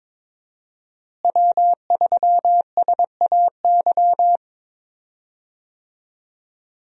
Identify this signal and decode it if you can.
Morse code, heard as a keyed tone, about 22 wpm, 700 Hz. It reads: W3SAY
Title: USA Call Signs - 5 Characters - Repeated 2x - 22wpm